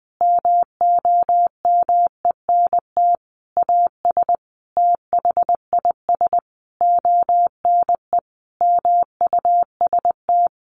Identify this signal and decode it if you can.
Morse code, heard as a keyed tone, about 20 wpm, 700 Hz. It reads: MOMENT AS THIS ONE MUST
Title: Sentences from Top 1000 Words 20wpm